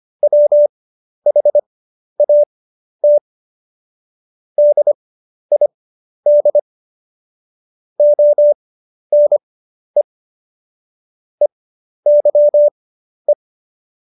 Morse code, heard as a keyed tone, about 25 wpm, spaced as for 11 wpm, 590 Hz. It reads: WHAT DID ONE EYE
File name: Jokes-25-char-10-eff